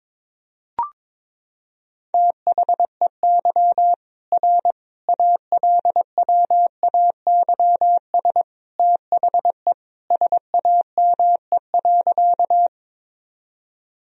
Morse code, heard as a keyed tone, about 22 wpm, 700 Hz. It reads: THEY R ALWAYS THE SAME.
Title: Sentences from Top 300 Words and Abbreviations 22wpm